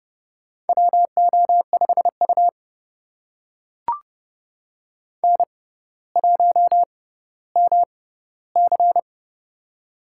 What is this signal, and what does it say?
Morse code, heard as a keyed tone, about 32 wpm, spaced as for 16 wpm, 700 Hz. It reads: WO5U N1MC
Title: USA Call Signs - 4 Characters - 6x Character Spacing - 30wpm